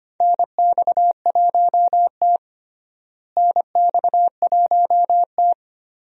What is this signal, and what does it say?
Morse code, heard as a keyed tone, about 25 wpm, 700 Hz. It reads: NX1T NX1T
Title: USA Call Signs - 4 Characters - Repeated 3x - 25wpm